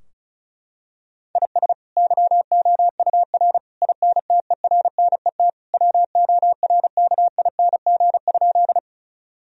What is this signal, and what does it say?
Morse code, heard as a keyed tone, about 36 wpm, 700 Hz. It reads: IS YOUR INTERNET WORKING?
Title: Sentences from Top 1000 Words - Review Part 4 - 35wpm